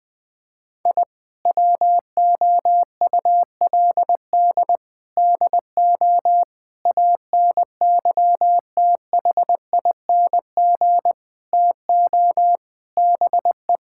Morse code, heard as a keyed tone, about 21 wpm, 700 Hz. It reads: I WOULD DO ANYTHING TO BE
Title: Sentences from Top 1000 Words - Review Part 6 - 20wpm